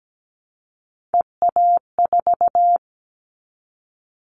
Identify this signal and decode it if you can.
Morse code, heard as a keyed tone, about 17 wpm, 700 Hz. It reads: EA4
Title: Callsign Prefixes for Top 25 DXCC - Repeated 3x - 17wpm